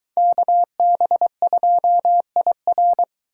Morse code, heard as a keyed tone, about 22 wpm, 700 Hz. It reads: KB2IR